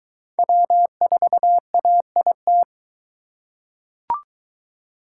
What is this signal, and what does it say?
Morse code, heard as a keyed tone, about 23 wpm, 700 Hz. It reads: W4AIT